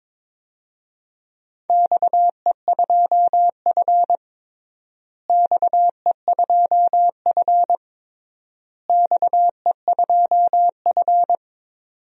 Morse code, heard as a keyed tone, about 22 wpm, 700 Hz. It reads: XE2F XE2F XE2F
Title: DXCC-100 Call Signs - 4 Characters - Repeated 3x - 22wpm